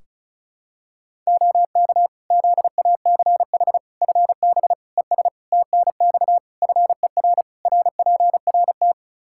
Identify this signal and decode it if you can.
Morse code, heard as a keyed tone, about 34 wpm, 700 Hz. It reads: OK ZACH FB ES TNX FER RPRT
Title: K1USN SST QSO Practice 35wpm